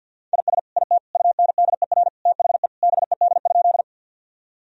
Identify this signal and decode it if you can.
Morse code, heard as a keyed tone, about 52 wpm, 700 Hz. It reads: IS IT UNDER THE BED?
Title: Sentences from Top 1000 Words - Review Part 1 - 50wpm